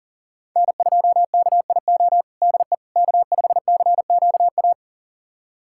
Morse code, heard as a keyed tone, about 40 wpm, 700 Hz. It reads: N2KIO DE K5CQA
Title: USA Callsign DE Callsign - 5 Characters - 40wpm